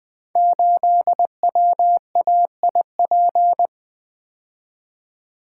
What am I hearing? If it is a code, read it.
Morse code, heard as a keyed tone, about 20 wpm, 700 Hz. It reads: 8WAIP
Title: Code Groups - 5 Letter:Numbers - 20wpm